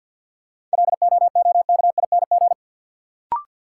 Morse code, heard as a keyed tone, about 52 wpm, 700 Hz. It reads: LOOKING
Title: Sentences from Top 1000 Words - Review Part 6 - New Word Drill - 50wpm